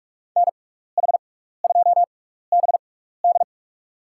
Morse code, heard as a keyed tone, about 44 wpm, spaced as for 15 wpm, 690 Hz. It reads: NH2BD
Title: USA Call Signs - 6x Character Spacing - 45wpm